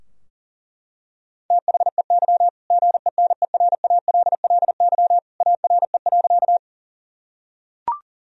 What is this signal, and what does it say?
Morse code, heard as a keyed tone, about 40 wpm, 700 Hz. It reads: THEY GENERALLY ARE.
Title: Sentences from Top 1000 Words - Review Part 4 - 40wpm